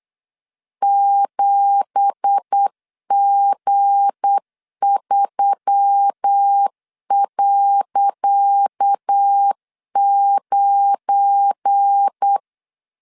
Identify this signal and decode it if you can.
Morse code, heard as a keyed tone, about 8 wpm, 800 Hz. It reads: 7G3.9
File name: LEZIONI / 10 SIGNS / 10C DOT 8wpm 3649